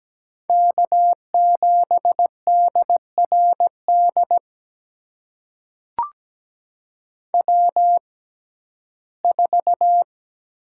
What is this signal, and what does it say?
Morse code, heard as a keyed tone, about 17 wpm, spaced as for 12 wpm, 700 Hz. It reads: K7DRD W4
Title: USA Call Signs - 5 Characters - 6x Character Spacing - 17wpm